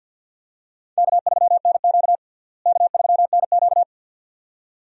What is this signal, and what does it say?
Morse code, heard as a keyed tone, about 45 wpm, 700 Hz. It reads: K3NQ K3NQ
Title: USA Call Signs - 4 Characters - Repeated 2x - 50wpm